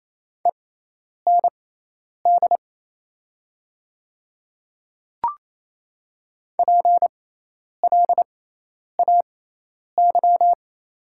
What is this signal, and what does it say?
Morse code, heard as a keyed tone, about 28 wpm, spaced as for 10 wpm, 700 Hz. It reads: END PLAY
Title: 6x Character Spacing - Top 200 Words - 28wpm